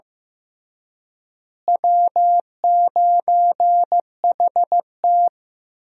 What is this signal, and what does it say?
Morse code, heard as a keyed tone, about 15 wpm, 700 Hz. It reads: W9HT
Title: USA Call Signs - 4 Characters - Repeated 3x - 15wpm